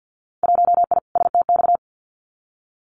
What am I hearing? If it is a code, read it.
Morse code, heard as a keyed tone, about 50 wpm, 700 Hz. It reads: 1I STX
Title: Winter Field Day Exchanges 50wpm